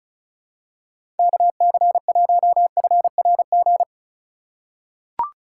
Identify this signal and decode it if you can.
Morse code, heard as a keyed tone, about 36 wpm, 700 Hz. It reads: KC1FRG
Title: USA Call Signs - 6 Characters - 4x Character Spacing - 35wpm